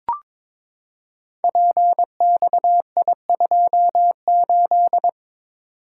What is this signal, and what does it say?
Morse code, heard as a keyed tone, about 22 wpm, 700 Hz. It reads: PXI28
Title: Code Groups - 5 Letter:Numbers - 22wpm